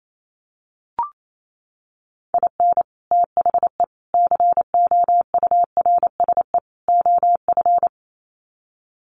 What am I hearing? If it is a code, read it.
Morse code, heard as a keyed tone, about 28 wpm, 700 Hz. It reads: IN THE COURSE OF
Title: Sets of 4 Words 28wpm